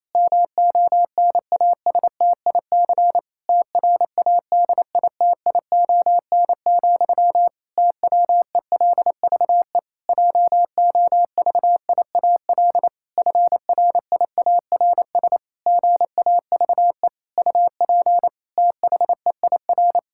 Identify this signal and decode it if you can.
Morse code, heard as a keyed tone, about 28 wpm, 700 Hz. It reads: MONASTIC TRADITION, TWELVE JOVIAL FRIARS GAVE UP THEIR
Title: Pangram Challenge v2 - 28wpm